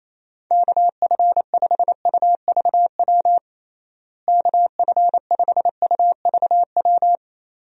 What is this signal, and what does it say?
Morse code, heard as a keyed tone, about 28 wpm, 700 Hz. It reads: KF5UVW KF5UVW
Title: USA Call Signs - 6 Characters - Repeated 2x - 28wpm